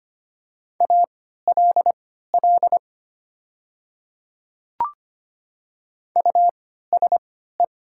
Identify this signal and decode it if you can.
Morse code, heard as a keyed tone, about 25 wpm, spaced as for 14 wpm, 700 Hz. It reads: ALL USE
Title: 3x Character Spacing - Top 100 Words - 25wpm